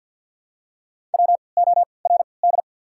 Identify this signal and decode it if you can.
Morse code, heard as a keyed tone, about 53 wpm, 690 Hz. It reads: W O R D
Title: 3x Character Spacing - Top 300 Words - 50wpm